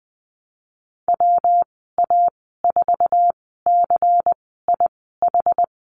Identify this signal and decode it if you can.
Morse code, heard as a keyed tone, about 20 wpm, spaced as for 15 wpm, 700 Hz. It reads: WA4CIH
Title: 140 Lesson 23g - Letter f - Call Signs 20wpm